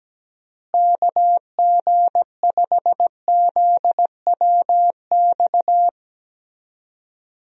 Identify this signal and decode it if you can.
Morse code, heard as a keyed tone, about 17 wpm, 700 Hz. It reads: KG5ZWX
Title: USA Call Signs - 6 Characters - Repeated 2x - 17wpm